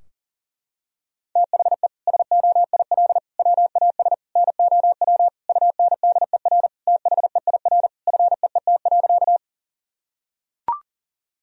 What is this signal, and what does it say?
Morse code, heard as a keyed tone, about 37 wpm, 700 Hz. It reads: THE SOIL WAS NOW UNDER THEIR FEET.
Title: Sentences from Top 1000 Words - Review Part 4 - 40wpm